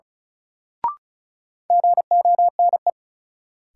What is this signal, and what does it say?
Morse code, heard as a keyed tone, about 36 wpm, 700 Hz. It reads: GONE
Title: Sentences from Top 1200 Words - New Word Drill - 35wpm